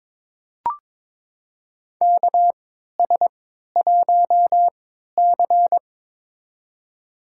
Morse code, heard as a keyed tone, about 22 wpm, spaced as for 13 wpm, 700 Hz. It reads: KS1C
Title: USA Call Signs - 4 Characters - 3x Character Spacing - 22wpm